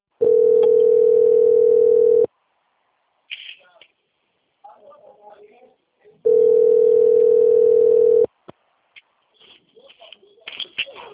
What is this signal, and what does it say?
Ringback tone (North America), 440 + 480 Hz, with speech.
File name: out- - - - - .26956